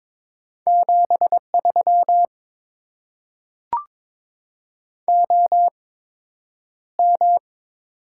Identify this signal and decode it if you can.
Morse code, heard as a keyed tone, about 22 wpm, spaced as for 9 wpm, 690 Hz. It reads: 73 OM
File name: Top 90 QSO Elements - 8x Character Spacing - 22wpm